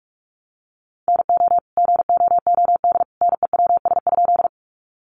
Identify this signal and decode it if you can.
Morse code, heard as a keyed tone, about 44 wpm, 700 Hz. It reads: NO GOOD NEWS?
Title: Sentences from Top 800 Words 45wpm